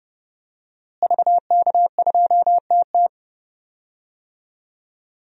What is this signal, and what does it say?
Morse code, heard as a keyed tone, about 30 wpm, 700 Hz. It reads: VK2TT
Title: DXCC-100 Call Signs - 5 Characters - Repeated 3x - 30wpm